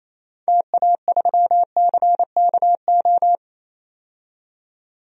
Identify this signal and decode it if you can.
Morse code, heard as a keyed tone, about 28 wpm, 700 Hz. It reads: TA3CKO
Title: DXCC-100 Call Signs - 6 Characters - 28wpm